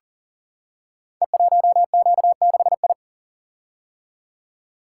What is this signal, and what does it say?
Morse code, heard as a keyed tone, about 39 wpm, 700 Hz. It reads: E1Q6I
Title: Code Groups - 5 Letter:Numbers - 40wpm